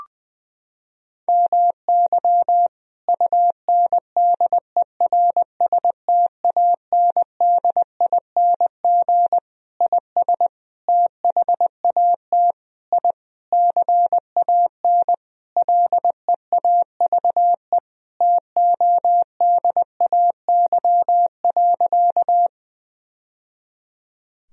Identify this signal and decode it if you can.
Morse code, heard as a keyed tone, about 21 wpm, 700 Hz. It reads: MY UNDERSTANDING IS THAT I CAN LEAVE TODAY.
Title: Sentences from Top 1000 Words - Review Part 3 - 20wpm